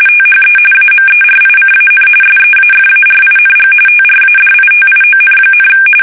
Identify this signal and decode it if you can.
RTTY, about 45.45 baud. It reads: THE AMERICAN SEAFARING TRADITION,
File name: fsk850 / usnavy1